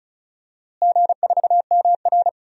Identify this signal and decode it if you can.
Morse code, heard as a keyed tone, about 36 wpm, 700 Hz. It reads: G4MR
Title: DXCC-100 Call Signs - 4 Characters - Repeated 3x - 35wpm